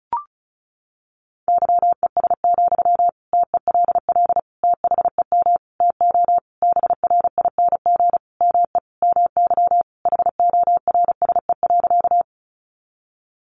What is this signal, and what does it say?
Morse code, heard as a keyed tone, about 34 wpm, 700 Hz. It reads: YES, TELL THEM TO BRING ME MY HORSE.
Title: Sentences from Top 500 Words 35wpm